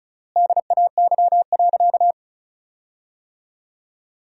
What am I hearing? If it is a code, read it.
Morse code, heard as a keyed tone, about 34 wpm, 700 Hz. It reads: DAY.
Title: 3x Word-Spacing - Sentences from Top 500 Words - 35wpm